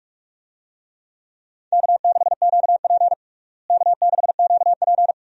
Morse code, heard as a keyed tone, about 46 wpm, 700 Hz. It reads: K6QP K6QP
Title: USA Call Signs - 4 Characters - Repeated 2x - 45wpm